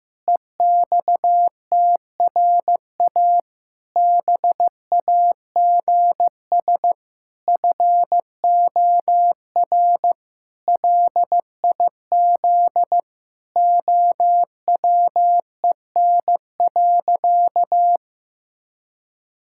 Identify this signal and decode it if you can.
Morse code, heard as a keyed tone, about 15 wpm, 700 Hz. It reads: EXTRA BAGS FOR LIZ OWEN.
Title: Pangram Challenge v2 - 15wpm